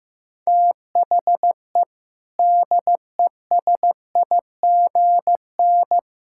Morse code, heard as a keyed tone, about 15 wpm, 700 Hz. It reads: THE DESIGN
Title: Sentences from Top 1000 Words - Review Part 5 - 15wpm